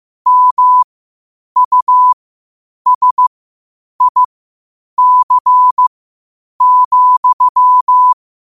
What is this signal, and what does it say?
Morse code, heard as a keyed tone, about 15 wpm, spaced as for 8 wpm, 1000 Hz. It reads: MUSIC,
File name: qotdfiles / 8wpm